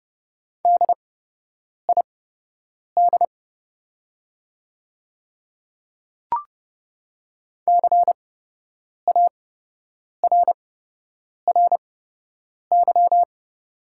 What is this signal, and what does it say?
Morse code, heard as a keyed tone, about 30 wpm, spaced as for 8 wpm, 700 Hz. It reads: DID CARRY
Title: 8x Character Spacing - Top 300 Words - 30wpm